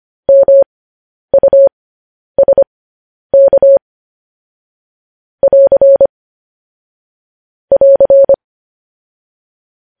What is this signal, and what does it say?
Morse code, heard as a keyed tone, about 25 wpm, spaced as for 10 wpm, 550 Hz. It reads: MUSK <AR> <AR>
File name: podcast / News headlines-1x-25x10